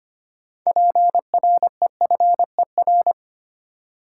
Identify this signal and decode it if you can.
Morse code, heard as a keyed tone, about 25 wpm, 700 Hz. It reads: PREFER
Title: Sentences from Top 1100 Words - New Word Drill - 25wpm